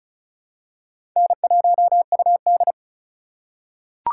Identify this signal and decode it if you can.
Morse code, heard as a keyed tone, about 36 wpm, 700 Hz. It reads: N1UD
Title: USA Call Signs - 2x Character Spacing - 35wpm